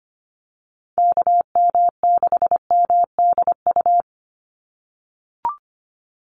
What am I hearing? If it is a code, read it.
Morse code, heard as a keyed tone, about 25 wpm, 700 Hz. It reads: KM6MDU
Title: USA Call Signs 25wpm